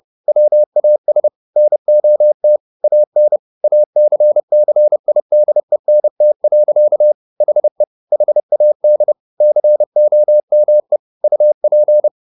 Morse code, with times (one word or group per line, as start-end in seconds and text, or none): 0.3-1.3 WAS
1.6-2.6 NOT
2.8-3.4 AN
3.6-7.1 ACCIDENT.
7.4-7.8 HE
8.1-9.1 HAD
9.4-11.0 COME
11.2-12.1 UP